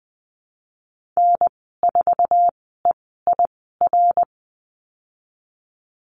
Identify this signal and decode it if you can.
Morse code, heard as a keyed tone, about 19 wpm, spaced as for 14 wpm, 700 Hz. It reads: N4EIR
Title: 245 Lesson 39g - Letter z - Call Signs 20wpm